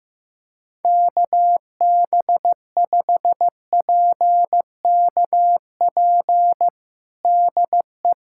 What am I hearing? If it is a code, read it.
Morse code, heard as a keyed tone, about 15 wpm, 700 Hz. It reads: KB5PKP DE
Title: USA Callsign DE Callsign - 6 Characters - 15wpm